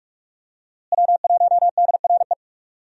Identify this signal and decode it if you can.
Morse code, heard as a keyed tone, about 43 wpm, 700 Hz. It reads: W1DRE
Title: USA Call Signs - 5 Characters - Repeated 2x - 45wpm